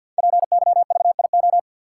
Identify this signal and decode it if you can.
Morse code, heard as a keyed tone, about 51 wpm, 700 Hz. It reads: PYVIO
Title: Code Groups - 5 Letter:Numbers - 50wpm